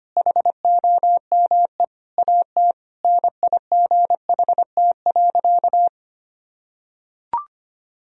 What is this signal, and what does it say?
Morse code, heard as a keyed tone, about 25 wpm, 700 Hz. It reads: HOME AT NIGHT.
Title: Sentences from Top 1000 Words - Review Part 3 - 25wpm